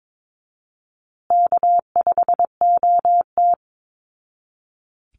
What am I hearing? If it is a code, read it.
Morse code, heard as a keyed tone, about 22 wpm, 700 Hz. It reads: K5OT
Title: Easy Call Signs Repeated 3x - 22wpm